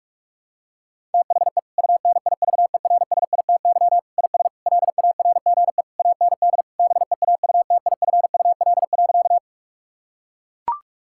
Morse code, heard as a keyed tone, about 42 wpm, 700 Hz. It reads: THE UNIVERSITY IS LARGE AND BEAUTIFUL.
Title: Sentences from Top 1000 Words - Review Part 5 - 45wpm